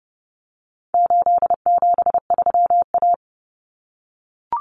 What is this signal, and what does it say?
Morse code, heard as a keyed tone, about 30 wpm, 700 Hz. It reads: 873A
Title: Common Cut Numbers - 4-Digit 30wpm